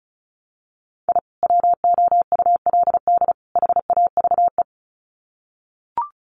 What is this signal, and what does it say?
Morse code, heard as a keyed tone, about 34 wpm, 700 Hz. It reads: I WOULD HAVE
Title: Sets of 3 Words 35wpm